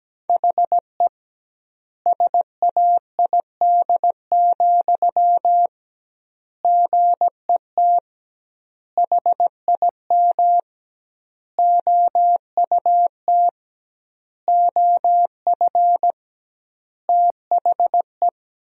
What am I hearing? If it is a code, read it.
Morse code, heard as a keyed tone, about 17 wpm, 700 Hz. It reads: HE SAID, GET HIM OUT OF THE
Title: 2x Word-Spacing - Sentences from Top 400 Words - 17wpm